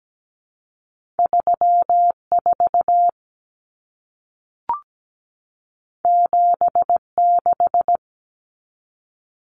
Morse code, heard as a keyed tone, about 17 wpm, 700 Hz. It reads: 34 76
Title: Common Cut Numbers - 2-Digit 17wpm